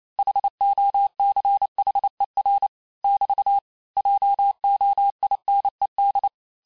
Morse code, heard as a keyed tone, about 28 wpm, 800 Hz. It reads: HOCHER <BT> JOINED
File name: cours-du-mercredi / audios / cours-865-teste-les-mots-vitesse-27mpm